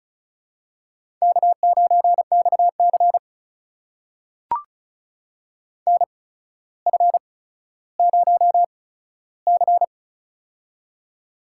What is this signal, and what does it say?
Morse code, heard as a keyed tone, about 34 wpm, spaced as for 15 wpm, 700 Hz. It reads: K9XC NF0C
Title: USA Call Signs - 4 Characters - 8x Character Spacing - 35wpm